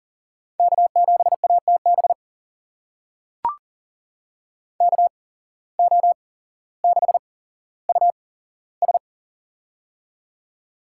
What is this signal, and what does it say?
Morse code, heard as a keyed tone, about 40 wpm, spaced as for 17 wpm, 700 Hz. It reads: K7ATB KO6US
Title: USA Call Signs - 5 Characters - 8x Character Spacing - 40wpm